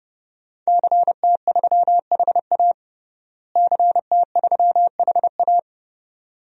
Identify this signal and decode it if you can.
Morse code, heard as a keyed tone, about 30 wpm, 700 Hz. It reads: CT3HA CT3HA
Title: DXCC-100 Call Signs - 5 Characters - Repeated 2x - 30wpm